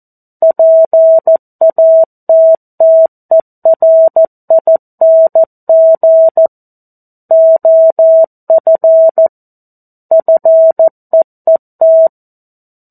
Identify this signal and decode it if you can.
Morse code, heard as a keyed tone, about 14 wpm, 650 Hz. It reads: PATTERING OF FEET
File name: 02-Chapter-02 14 WPM